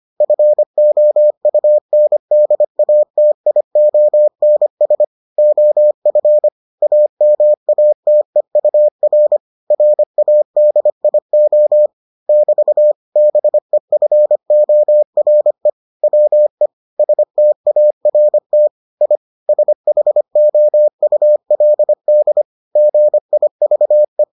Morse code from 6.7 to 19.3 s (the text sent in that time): AMATEUR RADIO <BT> BEFORE WE START I